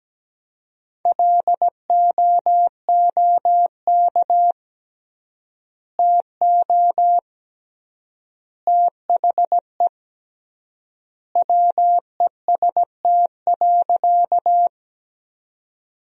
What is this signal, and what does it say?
Morse code, heard as a keyed tone, about 17 wpm, 700 Hz. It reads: LOOK TO THE WEST.
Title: 3x Word-Spacing - Sentences from Top 400 Words - 17wpm